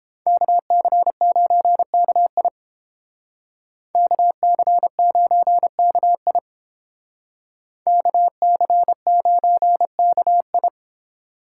Morse code, heard as a keyed tone, about 30 wpm, 700 Hz. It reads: KC9KI KC9KI KC9KI